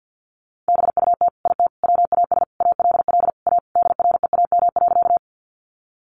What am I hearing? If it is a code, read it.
Morse code, heard as a keyed tone, about 50 wpm, 700 Hz. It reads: BUT IT WAS ALL A DREAM.
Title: Sentences from Top 1500 Words 50wpm